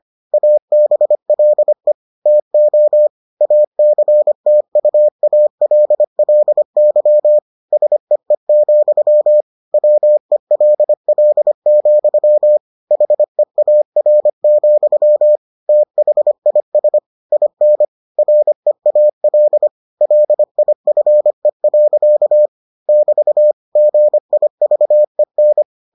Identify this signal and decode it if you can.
Morse code, heard as a keyed tone, about 25 wpm, 600 Hz. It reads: ABLE TO ACTUALLY SEE, WELL, HEAR, THIS IN REAL LIFE. <BT> GIVEN